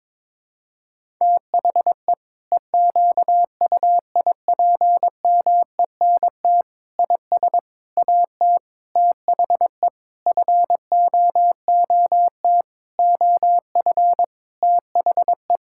Morse code, heard as a keyed tone, about 22 wpm, 700 Hz. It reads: THE EQUIPMENT IS AT THE FOOT OF THE
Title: Sentences from Top 1000 Words - Review Part 5 - 22wpm